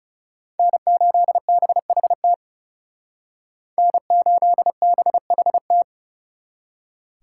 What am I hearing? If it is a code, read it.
Morse code, heard as a keyed tone, about 40 wpm, 700 Hz. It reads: N8BHT N8BHT